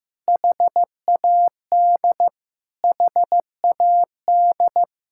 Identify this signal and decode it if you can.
Morse code, heard as a keyed tone, about 15 wpm, 700 Hz. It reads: HAD HAD
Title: Sentences from Top 1000 Words - Review Part 6 - 15wpm